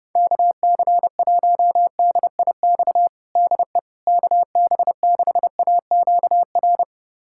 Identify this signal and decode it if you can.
Morse code, heard as a keyed tone, about 32 wpm, 700 Hz. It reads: KC1DIX DE KB6AQR
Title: USA Callsign DE Callsign 30wpm